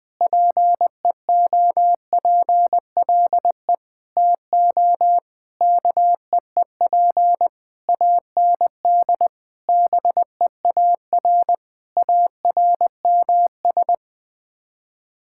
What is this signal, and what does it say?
Morse code, heard as a keyed tone, about 20 wpm, 700 Hz. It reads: PEOPLE TO KEEP AND BEAR ARMS
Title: Sets of 9 Words - 20wpm